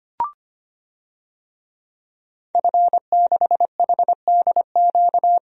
Morse code, heard as a keyed tone, about 25 wpm, 700 Hz. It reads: F6HDQ
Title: DXCC-100 Call Signs - 5 Characters - 25wpm